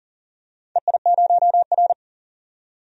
Morse code, heard as a keyed tone, about 40 wpm, 700 Hz. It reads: EI0R
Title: DXCC-100 Call Signs - 4 Characters - 40wpm